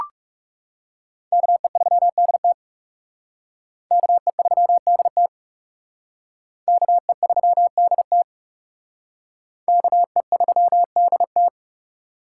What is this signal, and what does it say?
Morse code, heard as a keyed tone, about 40 wpm, 700 Hz. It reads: KE3DT KE3DT KE3DT KE3DT